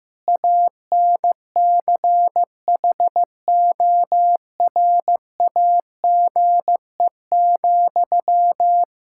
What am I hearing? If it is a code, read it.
Morse code, heard as a keyed tone, about 15 wpm, 700 Hz. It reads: ANCHORAGE,
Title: Practice QSO - Exchange 1 - Beginning - 15wpm